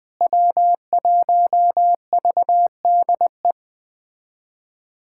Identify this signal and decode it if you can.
Morse code, heard as a keyed tone, about 20 wpm, 700 Hz. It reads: W1VDE
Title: USA Callsign DE Callsign - 5 Characters - 20wpm